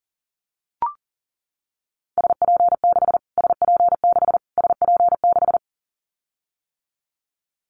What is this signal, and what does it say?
Morse code, heard as a keyed tone, about 40 wpm, 700 Hz. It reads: SP6 SP6 SP6
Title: Callsign Prefixes for Top 25 DXCC - Repeated 3x - 40wpm